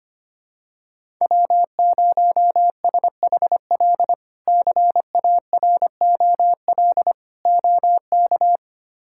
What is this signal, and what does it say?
Morse code, heard as a keyed tone, about 25 wpm, 700 Hz. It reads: W0SHL CAROL OK
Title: K1USN SST QSO Practice - 25wpm